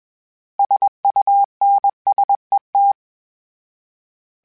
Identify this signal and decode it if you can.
Morse code, heard as a keyed tone, about 21 wpm, 800 Hz. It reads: SUNSET
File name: fromSMs / FromSMs mp3s / Cities@20